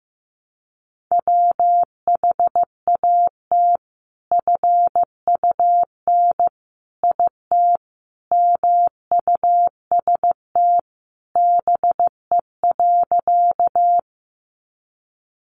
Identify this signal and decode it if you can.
Morse code, heard as a keyed tone, about 15 wpm, 700 Hz. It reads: WHAT FUN IT MUST BE.
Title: Sentences from Top 700 Words 15wpm